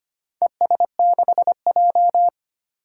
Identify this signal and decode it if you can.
Morse code, heard as a keyed tone, about 25 wpm, 700 Hz. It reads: ES6J
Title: Four Challenging Letter-Number Combinations 25wpm